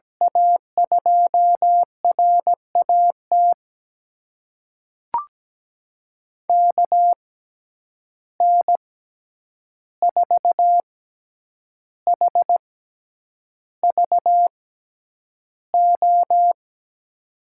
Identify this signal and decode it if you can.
Morse code, heard as a keyed tone, about 17 wpm, spaced as for 8 wpm, 700 Hz. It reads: A2RAT KN4HVO
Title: USA Call Signs - 6 Characters - 6x Character Spacing - 17wpm